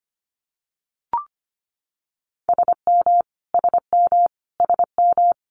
Morse code, heard as a keyed tone, about 25 wpm, 700 Hz. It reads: SM SM SM
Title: Multi-letter Phonemes Repeated 3x 25wpm